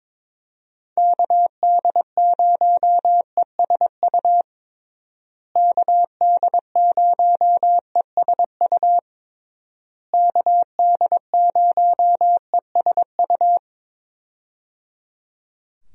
Morse code, heard as a keyed tone, about 22 wpm, 700 Hz. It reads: KD0ESU KD0ESU KD0ESU
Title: USA Call Signs - 6 Characters - Repeated 3x - 22wpm